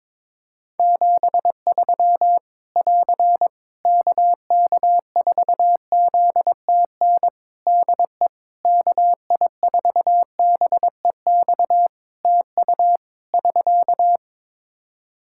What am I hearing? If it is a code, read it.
Morse code, heard as a keyed tone, about 22 wpm, 700 Hz. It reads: 73 <AR> KK4ZTN DE KI4BEX TU <SK>
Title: K1USN SST QSO Practice 22wpm